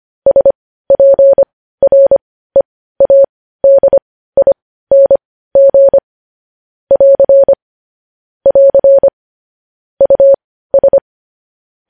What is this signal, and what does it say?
Morse code, heard as a keyed tone, about 25 wpm, spaced as for 15 wpm, 550 Hz. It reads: SPREADING <AR> <AR> US
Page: news headlines in morse code at 15 wpm